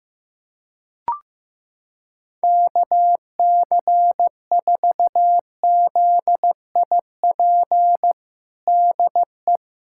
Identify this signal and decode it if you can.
Morse code, heard as a keyed tone, about 15 wpm, 700 Hz. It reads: KC4ZIP DE
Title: USA Callsign DE Callsign 15wpm